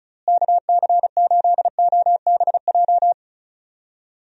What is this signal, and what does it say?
Morse code, heard as a keyed tone, about 36 wpm, 700 Hz. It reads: KC8OBJ
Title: USA Call Signs - 6 Characters - Repeated 2x - 35wpm